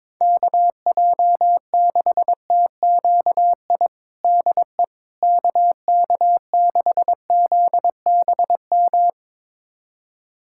Morse code, heard as a keyed tone, about 22 wpm, 700 Hz. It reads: KJ6TQI DE KK6ZBM
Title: USA Callsign DE Callsign 22wpm